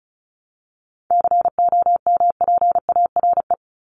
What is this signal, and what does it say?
Morse code, heard as a keyed tone, about 34 wpm, 700 Hz. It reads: COMPARE
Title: ED Suffix - Top 100 Words with Base Words 35wpm